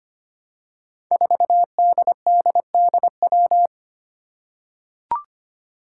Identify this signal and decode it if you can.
Morse code, heard as a keyed tone, about 25 wpm, 700 Hz. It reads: 4DDDW
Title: Five Challenging Letter-Number Combinations 25wpm